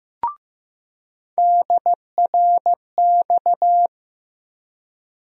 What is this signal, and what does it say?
Morse code, heard as a keyed tone, about 15 wpm, 700 Hz. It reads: DRX
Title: USA Call Sign Postfixes - 3 Letters 15wpm